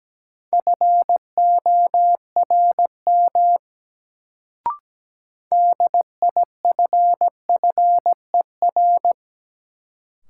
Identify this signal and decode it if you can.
Morse code, heard as a keyed tone, about 17 wpm, 700 Hz. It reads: FORM DIFFER
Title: Sentences from Top 1000 Words - Review Part 1 - New Word Drill - 17wpm